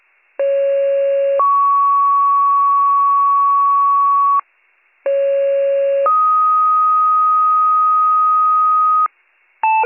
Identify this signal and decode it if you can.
Two-tone page, heard as a one-second tone, then a three-second tone, about 550 then 1120 Hz, about 550 then 1250 Hz. It is heard from 0.4 to 9.1 s.